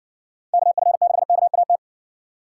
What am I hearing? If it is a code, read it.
Morse code, heard as a keyed tone, about 60 wpm, 700 Hz. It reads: KF6KAT